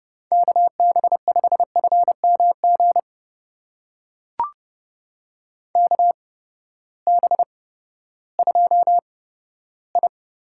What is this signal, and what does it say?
Morse code, heard as a keyed tone, about 31 wpm, spaced as for 15 wpm, 700 Hz. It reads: KB5FMG KB2I
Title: USA Call Signs - 6 Characters - 8x Character Spacing - 30wpm